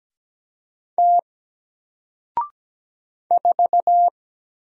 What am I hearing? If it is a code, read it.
Morse code, heard as a keyed tone, about 17 wpm, 700 Hz. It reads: T 4
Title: Single Letter-Number Alt 17wpm